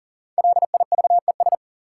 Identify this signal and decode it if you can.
Morse code, heard as a keyed tone, about 37 wpm, 700 Hz. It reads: LIVES
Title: Sets of 5 Words Encore - New Word Drill - 40wpm